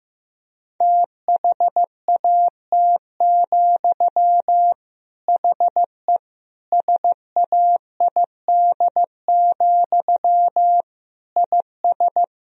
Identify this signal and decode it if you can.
Morse code, heard as a keyed tone, about 15 wpm, 700 Hz. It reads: THAT, HE SAID, IS
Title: Sentences from Top 1000 Words - Review Part 3 - 15wpm